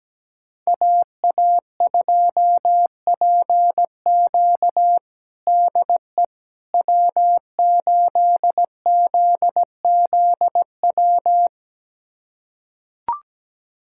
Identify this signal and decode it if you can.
Morse code, heard as a keyed tone, about 17 wpm, 700 Hz. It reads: AA2PQ DE W8ZZW
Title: USA Callsign DE Callsign 17wpm